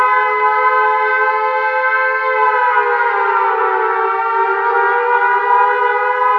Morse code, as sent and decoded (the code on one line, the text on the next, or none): none